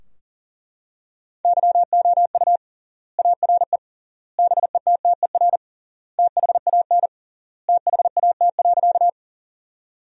Morse code, heard as a keyed tone, about 40 wpm, 700 Hz. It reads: YOU ARE BETTER THAN THAT.
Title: 3x Word-Spacing - Sentences from Top 400 Words - 40wpm